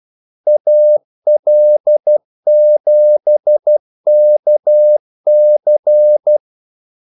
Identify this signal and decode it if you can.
Morse code, heard as a keyed tone, about 12 wpm, 600 Hz. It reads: AL7KC
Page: 2 ekstra ordmellemrum mellem kaldesignalerne